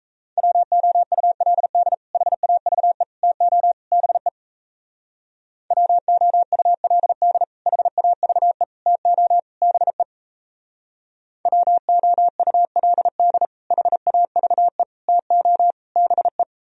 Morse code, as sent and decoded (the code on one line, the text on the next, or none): .-- --- ..- .-.. -.. / .... .- ...- . / - --- / -... / .-- --- ..- .-.. -.. / .... .- ...- . / - --- / -... / .-- --- ..- .-.. -.. / .... .- ...- . / - --- / -... .
WOULD HAVE TO BE WOULD HAVE TO BE WOULD HAVE TO BE